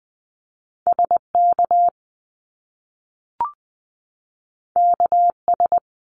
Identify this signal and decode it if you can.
Morse code, heard as a keyed tone, about 20 wpm, 700 Hz. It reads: SK KS
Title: Top Two Letter Combinations in Words 20wpm